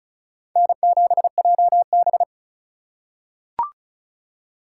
Morse code, heard as a keyed tone, about 34 wpm, 700 Hz. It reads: N7JB
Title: USA Call Signs - 4 Characters - 3x Character Spacing - 35wpm